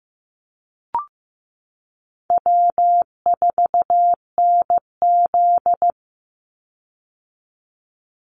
Morse code, heard as a keyed tone, about 15 wpm, 700 Hz. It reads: W4NZ
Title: Easy Call Signs - 15wpm